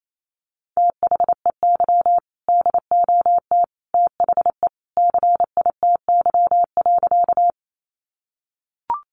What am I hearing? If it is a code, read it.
Morse code, heard as a keyed tone, about 28 wpm, 700 Hz. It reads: THEY DOT THE CITY.
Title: Sentences from Top 1500 Words 28wpm